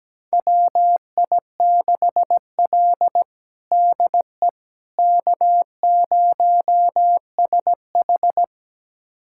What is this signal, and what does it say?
Morse code, heard as a keyed tone, about 17 wpm, 700 Hz. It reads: WI6L DE K0SH
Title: USA Callsign DE Callsign - 4 Characters - 17wpm